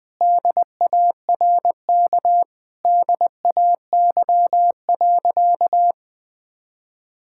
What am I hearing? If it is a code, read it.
Morse code, heard as a keyed tone, about 21 wpm, 700 Hz. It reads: DARK DAY.
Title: Sentences from Top 1000 Words - Review Part 5 - 20wpm